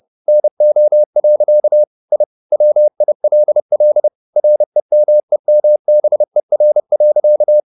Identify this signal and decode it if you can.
Morse code, heard as a keyed tone, about 30 wpm, 600 Hz. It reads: NO. I WILL REMEMBER.